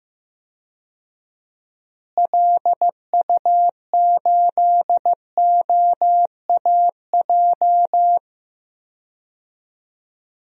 Morse code, heard as a keyed tone, about 16 wpm, 700 Hz. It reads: LU8OAJ
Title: DXCC-100 Call Signs - 6 Characters - Repeated 2x - 15wpm